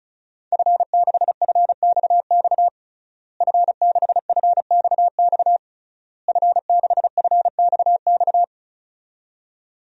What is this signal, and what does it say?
Morse code, heard as a keyed tone, about 34 wpm, 700 Hz. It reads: F6FXX F6FXX F6FXX
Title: DXCC-100 Call Signs - 5 Characters - Repeated 3x - 35wpm